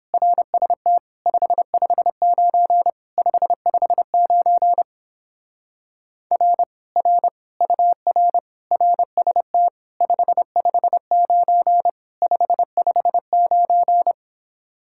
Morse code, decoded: RST 559 559 R R UR RST 559 559

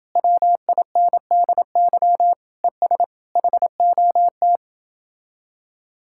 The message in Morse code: WINDY ES HOT

